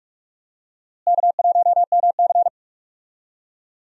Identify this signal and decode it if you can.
Morse code, heard as a keyed tone, about 41 wpm, 700 Hz. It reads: K1MC
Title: USA Call Signs - 4 Characters - Repeated 3x - 45wpm